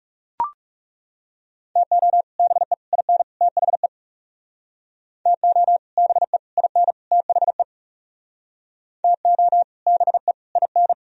Morse code, heard as a keyed tone, about 40 wpm, 700 Hz. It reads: TO BE IN THE TO BE IN THE TO BE IN